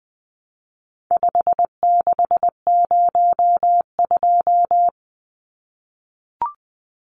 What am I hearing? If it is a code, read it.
Morse code, heard as a keyed tone, about 20 wpm, 700 Hz. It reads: 5602
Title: Numbers - 4-Digit 20wpm